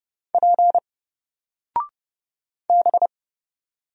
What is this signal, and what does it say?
Morse code, heard as a keyed tone, about 29 wpm, 700 Hz. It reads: P B
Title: Single Character (Letter:Number:Punctuation:Prosign) 30wpm